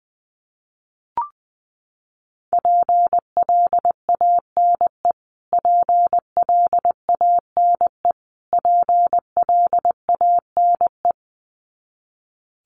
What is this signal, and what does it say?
Morse code, heard as a keyed tone, about 20 wpm, 700 Hz. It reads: PLANE PLANE PLANE
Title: Top 500 Words Repeated 20wpm